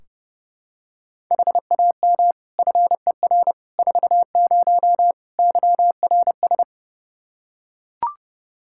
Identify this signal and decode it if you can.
Morse code, heard as a keyed tone, about 30 wpm, 700 Hz. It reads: HAM FER 40 YRS
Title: Practice QSO - Exchange 3 - More Info - 30wpm